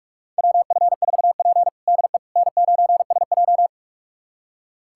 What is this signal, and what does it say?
Morse code, heard as a keyed tone, about 42 wpm, 700 Hz. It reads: WF4P DE N9SJ
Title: USA Callsign DE Callsign - 4 Characters - 45wpm